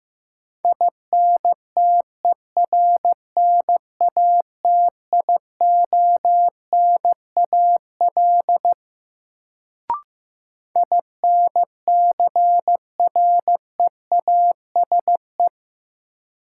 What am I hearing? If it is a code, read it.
Morse code, heard as a keyed tone, about 15 wpm, 700 Hz. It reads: INTERNATIONAL INCREASE
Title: Sentences from Top 1000 Words - Review Part 4 - New Word Drill - 15wpm